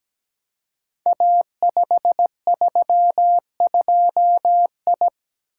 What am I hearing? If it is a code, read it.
Morse code, heard as a keyed tone, about 17 wpm, 700 Hz. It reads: A532I
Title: Code Groups - 5 Letter:Numbers - 17wpm